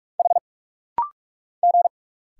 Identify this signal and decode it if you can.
Morse code, heard as a keyed tone, about 46 wpm, 700 Hz. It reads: H G
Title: Single Letters - Rapid-Fire 45wpm